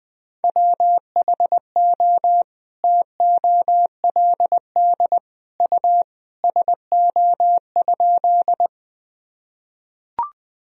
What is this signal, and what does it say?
Morse code, heard as a keyed tone, about 20 wpm, 700 Hz. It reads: WHO TOLD U SO?
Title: Sentences from Top 400 Words and Abbreviations 20wpm